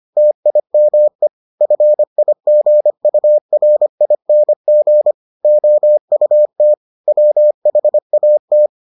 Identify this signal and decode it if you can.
Morse code, heard as a keyed tone, about 25 wpm, 600 Hz. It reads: TIME FIGURING OUT WHAT